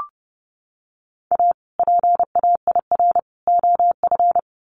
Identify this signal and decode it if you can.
Morse code, heard as a keyed tone, about 30 wpm, 700 Hz. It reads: A PAIR OF
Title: Sets of 3 Words 30wpm